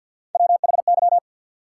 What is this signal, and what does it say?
Morse code, heard as a keyed tone, about 48 wpm, 700 Hz. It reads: WHY
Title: Three Letter Words - 50wpm